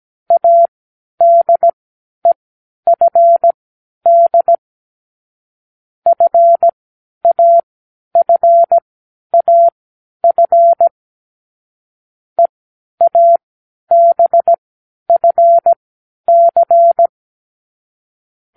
Morse code, heard as a keyed tone, about 17 wpm, spaced as for 10 wpm, 680 Hz. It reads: ADEFD FAFAF EABFC
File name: k7qo / Lesson9(A-F)